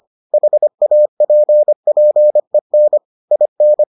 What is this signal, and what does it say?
Morse code, heard as a keyed tone, about 25 wpm, 600 Hz. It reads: HAPPEN IN